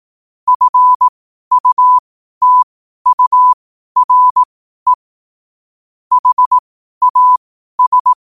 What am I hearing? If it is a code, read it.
Morse code, heard as a keyed tone, about 18 wpm, spaced as for 12 wpm, 1000 Hz. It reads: FUTURE HAS
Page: New quotes every day in morse code at 12 Words per minute.